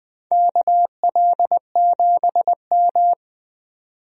Morse code, heard as a keyed tone, about 20 wpm, 700 Hz. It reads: KL7M
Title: DXCC-100 Call Signs - 4 Characters - 20wpm